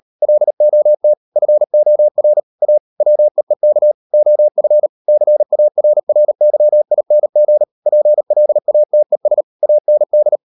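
Morse code, heard as a keyed tone, about 38 wpm, 600 Hz. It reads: LOT FOR A WEEK OF CARRYING PLATES AND
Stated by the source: Morse code practice at 38 WPM, sending Ken Follett's novel Never in English, from CW med Gnister.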